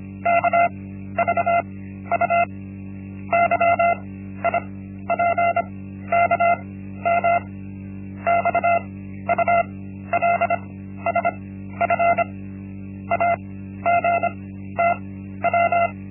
Morse code, heard as a keyed tone, about 25 wpm, 690 Hz. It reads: K V U Y I P K M X U L S F A G T W
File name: Linksmakalnis-1